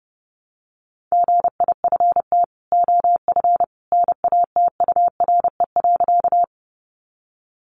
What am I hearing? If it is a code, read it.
Morse code, heard as a keyed tone, about 30 wpm, 700 Hz. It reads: GIFT OF NATURE.
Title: Sentences from Top 900 Words 30wpm